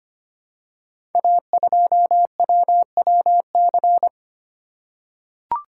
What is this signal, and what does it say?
Morse code, heard as a keyed tone, about 25 wpm, 700 Hz. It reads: A2WWC
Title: Code Groups - 5 Letter:Numbers - 25wpm